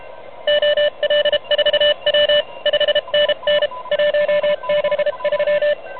Morse code, heard as a keyed tone, about 30 wpm, 590 Hz, with noise. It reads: OL4W 5NN 163
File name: truSDX